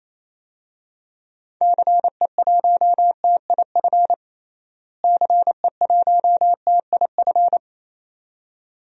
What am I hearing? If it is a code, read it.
Morse code, heard as a keyed tone, about 28 wpm, 700 Hz. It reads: CE1TIF CE1TIF
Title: DXCC-100 Call Signs - 6 Characters - Repeated 2x - 28wpm